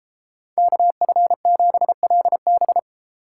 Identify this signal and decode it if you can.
Morse code, heard as a keyed tone, about 30 wpm, 700 Hz. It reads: KF7LB